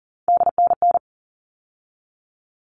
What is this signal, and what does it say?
Morse code, heard as a keyed tone, about 40 wpm, 700 Hz. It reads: DNN
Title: All Cut Numbers - 3-Digit 40wpm